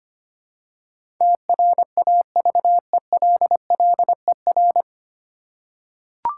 Morse code, heard as a keyed tone, about 25 wpm, 700 Hz. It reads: TRAVELLER
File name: practice / 25wpm / 9 Letter Words - First 3 Letters Given - 25wpm